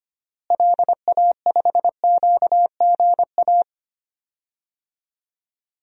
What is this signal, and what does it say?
Morse code, heard as a keyed tone, about 25 wpm, 700 Hz. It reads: LA5QGA
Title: DXCC-100 Call Signs - 6 Characters - Repeated 2x - 25wpm